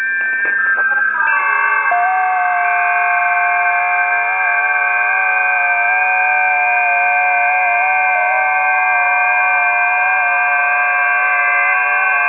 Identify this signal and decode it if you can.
FT8.
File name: 20m busy / test 19